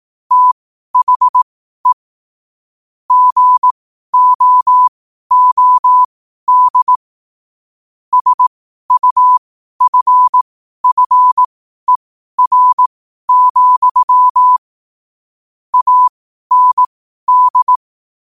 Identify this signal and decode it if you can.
Morse code, heard as a keyed tone, about 18 wpm, spaced as for 12 wpm, 1000 Hz. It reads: THE GOOD SUFFER, AND
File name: qotdfiles / 12wpm